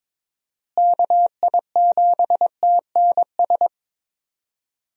Morse code, heard as a keyed tone, about 22 wpm, 700 Hz. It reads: KI7TNS
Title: USA Call Signs - 6 Characters - Repeated 3x - 22wpm